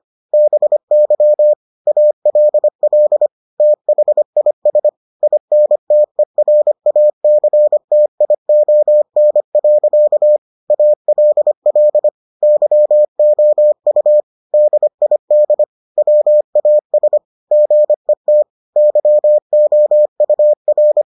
Morse code, heard as a keyed tone, about 25 wpm, 600 Hz. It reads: BY ALL THIS INTERACTION. ALL YOU DID WAS GET YOUR